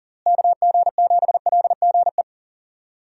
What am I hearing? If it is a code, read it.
Morse code, heard as a keyed tone, about 40 wpm, 700 Hz. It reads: KG7LGE
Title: USA Call Signs - 6 Characters - Repeated 3x - 40wpm